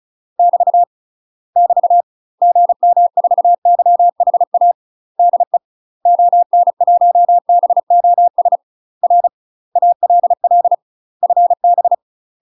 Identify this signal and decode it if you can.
Morse code, heard as a keyed tone, about 36 wpm, 700 Hz. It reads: <BT> <BT> GM4YHA DE ON1BOS R ALL FB
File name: UK MASTER QSOs TRAINING / 35WPM / UK MASTER QSOs 0012